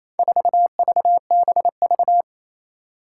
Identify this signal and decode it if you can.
Morse code, heard as a keyed tone, about 28 wpm, 700 Hz. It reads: 4VBV
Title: Four Challenging Letter-Number Combinations 28wpm